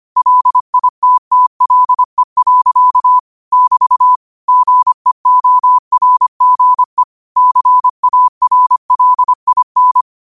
Morse code, heard as a keyed tone, about 25 wpm, 1000 Hz. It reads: LITTLE. <BT> GEORGE CARLIN
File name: qotdfiles / 25wpm